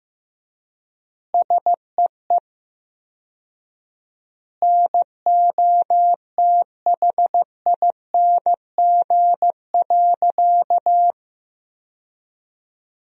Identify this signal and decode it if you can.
Morse code, heard as a keyed tone, about 16 wpm, 700 Hz. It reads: SEE NOTHING.
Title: 4x Word-Spacing - Sentences from Top 500 Words - 15wpm